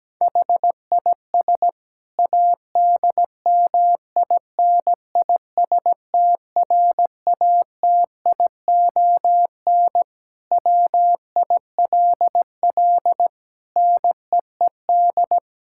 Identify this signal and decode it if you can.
Morse code, heard as a keyed tone, about 17 wpm, 700 Hz. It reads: HIS ADMINISTRATION WILL NEED
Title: Sentences from Top 1600 Words 17wpm